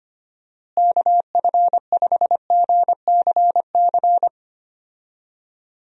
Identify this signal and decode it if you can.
Morse code, heard as a keyed tone, about 25 wpm, 700 Hz. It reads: KF5GCC
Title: USA Call Signs - 6 Characters - Repeated 2x - 25wpm